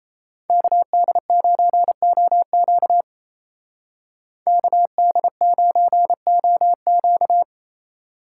Morse code, heard as a keyed tone, about 34 wpm, 700 Hz. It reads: KD9OQ KD9OQ
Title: Speed Racing - USA 2x2 Call Signs 28wpm